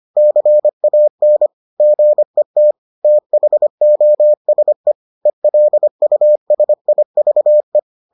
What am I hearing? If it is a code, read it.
Morse code, heard as a keyed tone, about 25 wpm, 600 Hz. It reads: CAN GET THOSE ELUSIVE